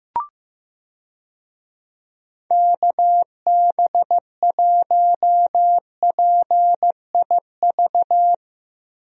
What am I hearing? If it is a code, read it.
Morse code, heard as a keyed tone, about 15 wpm, 700 Hz. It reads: KB1PIV
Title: USA Call Signs - 6 Characters - 15wpm